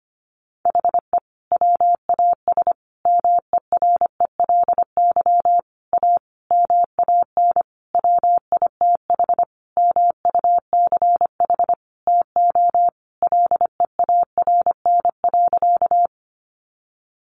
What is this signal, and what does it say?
Morse code, heard as a keyed tone, about 25 wpm, 700 Hz. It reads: HE WAS MERELY A MAN WITH MUCH TO LEARN.
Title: Sentences from Top 1400 Words 25wpm